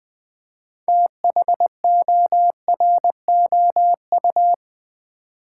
Morse code, heard as a keyed tone, about 20 wpm, 700 Hz. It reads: THOROU
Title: Build A Word - 2 Letters At A Time - 8-Letter Words - 20wpm